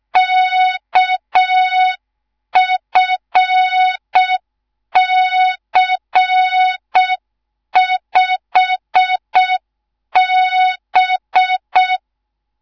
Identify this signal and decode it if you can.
Morse code, heard as a keyed tone, about 6 wpm, 3700 Hz. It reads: KFC5B
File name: Morze-HA2MN / hang / 10GWZ HA2MN